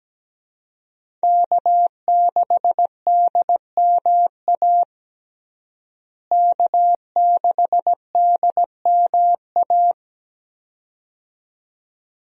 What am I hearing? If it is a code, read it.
Morse code, heard as a keyed tone, about 17 wpm, 700 Hz. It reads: K6DMA K6DMA
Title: USA Call Signs - 5 Characters - Repeated 2x - 17wpm